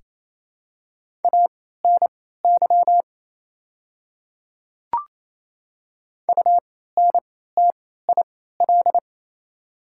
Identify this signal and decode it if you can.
Morse code, heard as a keyed tone, about 28 wpm, spaced as for 16 wpm, 700 Hz. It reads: ANY UNTIL
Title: 3x Character Spacing - Top 300 Words - 28wpm